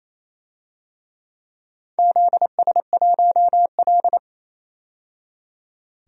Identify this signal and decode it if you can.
Morse code, heard as a keyed tone, about 28 wpm, 700 Hz. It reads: ZS1L
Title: DXCC-100 Call Signs - 4 Characters - 28wpm